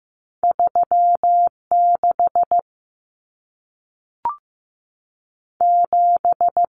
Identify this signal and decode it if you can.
Morse code, heard as a keyed tone, about 15 wpm, spaced as for 3 wpm, 700 Hz. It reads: 3 67
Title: Numbers - 2-Digit 15wpm